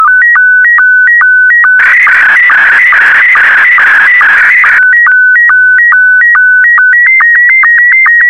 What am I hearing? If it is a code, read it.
SSTV (slow-scan television), Scottie 1.